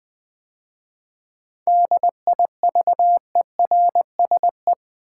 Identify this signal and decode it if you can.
Morse code, heard as a keyed tone, about 20 wpm, 700 Hz. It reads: DIVERSE
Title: 7 Letter Words - First 3 Letters Given - 20wpm